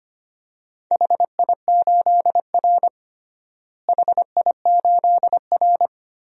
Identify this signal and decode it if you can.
Morse code, heard as a keyed tone, about 25 wpm, 700 Hz. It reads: HI8R HI8R
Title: DXCC-100 Call Signs - 4 Characters - Repeated 2x - 25wpm